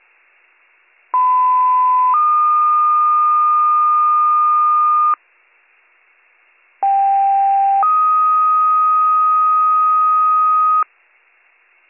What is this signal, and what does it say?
Two-tone page, heard as a one-second tone, then a three-second tone, about 1010 then 1220 Hz, about 790 then 1250 Hz. It is heard from 1.1 to 10.8 s.